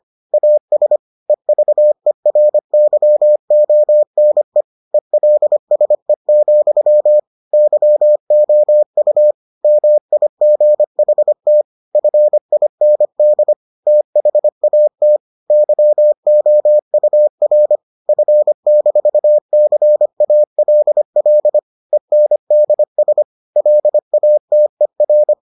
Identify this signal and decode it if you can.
Morse code, heard as a keyed tone, about 25 wpm, 600 Hz. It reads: AS EVERYONE ELSE, YOU MIGHT FIND THAT YOUR F-CALL ENDS LATER